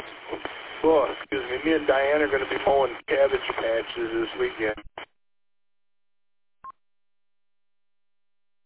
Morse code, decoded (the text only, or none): none